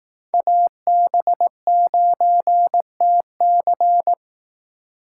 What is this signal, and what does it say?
Morse code, heard as a keyed tone, about 18 wpm, 700 Hz. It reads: AB9TC
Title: Speed Racing - USA 2x2 Call Signs 15wpm